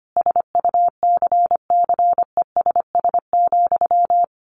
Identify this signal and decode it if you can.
Morse code, heard as a keyed tone, about 25 wpm, 700 Hz. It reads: SUCCESS,
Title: A Sound Of Thunder 25wpm